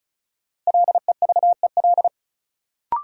Morse code, heard as a keyed tone, about 36 wpm, 700 Hz. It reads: LEVEL
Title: Sentences from Top 700 Words - New Word Drill - 35wpm